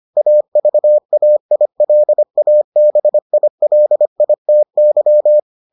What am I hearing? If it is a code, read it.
Morse code, heard as a keyed tone, about 25 wpm, 600 Hz. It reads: AVAILABILITY